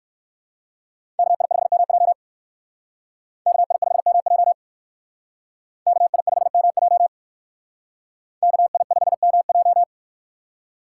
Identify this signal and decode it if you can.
Morse code, heard as a keyed tone, about 60 wpm, 700 Hz. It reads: KI5MJ KI5MJ KI5MJ KI5MJ